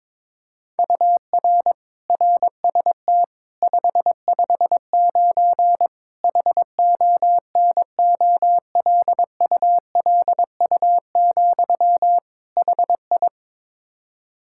Morse code, heard as a keyed tone, about 22 wpm, 700 Hz. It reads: UR RST 559 HONOLULU, HI
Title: Practice QSO - Exchange 1 - Beginning - 22wpm